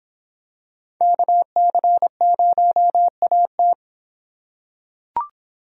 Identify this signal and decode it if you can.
Morse code, heard as a keyed tone, about 26 wpm, 700 Hz. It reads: KC0AT